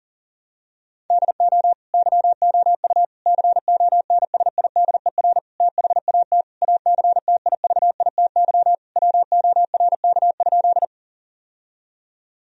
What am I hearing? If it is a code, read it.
Morse code, heard as a keyed tone, about 40 wpm, 700 Hz. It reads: DO YOU CONSIDER THAT ACTIVITY WORK?
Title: Sentences from Top 1000 Words - Review Part 4 - 40wpm